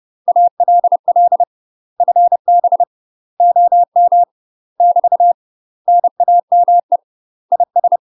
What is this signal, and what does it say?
Morse code, heard as a keyed tone, about 30 wpm, 700 Hz. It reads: ALL FB OM <BT> NAME IS
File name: UK MASTER QSOs TRAINING / 30WPM / UK MASTER QSOs 0036